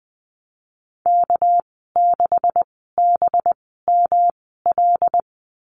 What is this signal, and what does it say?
Morse code, heard as a keyed tone, about 20 wpm, spaced as for 15 wpm, 700 Hz. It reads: K6BML
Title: 200 Lesson 32g - Letter k - Call Signs 20wpm